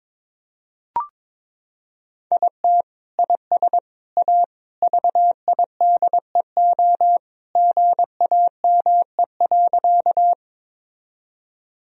Morse code, heard as a keyed tone, about 22 wpm, 700 Hz. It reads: IT IS A VIDEO GAME.
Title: Sentences from Top 1000 Words - Review Part 4 - 22wpm